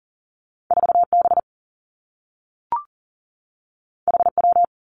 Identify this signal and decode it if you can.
Morse code, heard as a keyed tone, about 40 wpm, 700 Hz. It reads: 4 B H W
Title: Two Challenging Letter-Number Combinations 40wpm